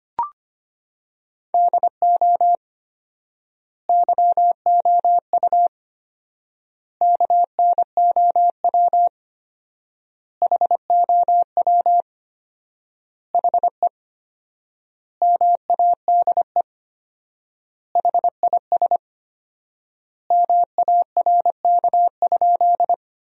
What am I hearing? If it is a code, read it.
Morse code, heard as a keyed tone, about 25 wpm, 700 Hz. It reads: DO YOU KNOW HOW HE MADE HIS MARK?
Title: 4x Word-Spacing - Sentences from Top 300 Words - 25wpm